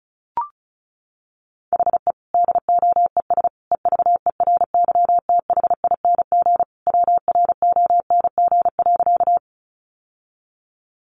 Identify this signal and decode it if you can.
Morse code, heard as a keyed tone, about 34 wpm, 700 Hz. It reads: HE DOES EVERYTHING WRONG.
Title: Sentences from Top 900 Words 35wpm